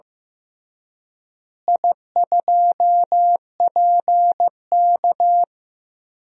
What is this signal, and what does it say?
Morse code, heard as a keyed tone, about 15 wpm, 700 Hz. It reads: I2PK
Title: DXCC-100 Call Signs - 4 Characters - Repeated 3x - 15wpm